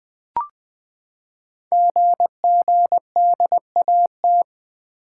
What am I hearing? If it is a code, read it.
Morse code, heard as a keyed tone, about 20 wpm, 700 Hz. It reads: GGDAT
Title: Code Groups - 5 Letters - 20wpm